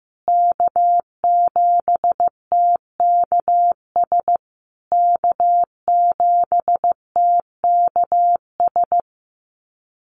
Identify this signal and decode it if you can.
Morse code, heard as a keyed tone, about 15 wpm, 700 Hz. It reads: K7TKS K7TKS
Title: USA Call Signs Repeated 3x 15wpm